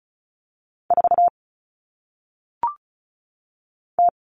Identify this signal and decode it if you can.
Morse code, heard as a keyed tone, about 36 wpm, 700 Hz. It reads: 4 T
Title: Common Cut Numbers - 1-Digit 35wpm